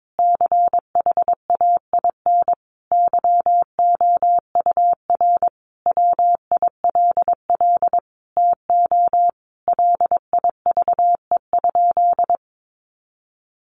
Morse code, heard as a keyed tone, about 22 wpm, 700 Hz. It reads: CHAIN YOUR WILL TO LIVE?
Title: Sentences from Top 1500 Words 22wpm